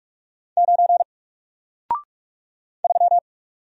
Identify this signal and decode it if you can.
Morse code, heard as a keyed tone, about 44 wpm, 700 Hz. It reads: 9 3
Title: Single Letter-Number Alt 45wpm